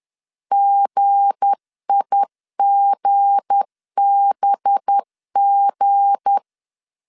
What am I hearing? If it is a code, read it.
Morse code, heard as a keyed tone, about 10 wpm, 800 Hz. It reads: GIGBG
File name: LEZIONI / 7 GF9Z / 7 G 10wpm 2501